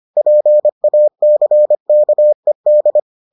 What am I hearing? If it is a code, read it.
Morse code, heard as a keyed tone, about 25 wpm, 600 Hz. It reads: PACKED